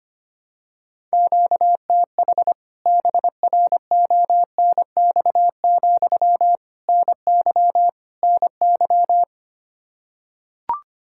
Morse code, heard as a keyed tone, about 25 wpm, 700 Hz. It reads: QTH BRONX, NY NY
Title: K1USN SST QSO Practice 25wpm